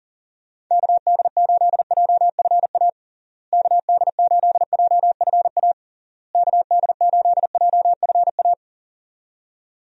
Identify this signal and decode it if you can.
Morse code, heard as a keyed tone, about 40 wpm, 700 Hz. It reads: KD8JFA KD8JFA KD8JFA
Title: USA Call Signs - 6 Characters - Repeated 3x - 40wpm